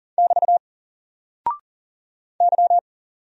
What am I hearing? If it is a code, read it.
Morse code, heard as a keyed tone, about 40 wpm, 700 Hz. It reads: <BT> Y
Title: Single Character (Letter:Number:Punctuation:Prosign) 40wpm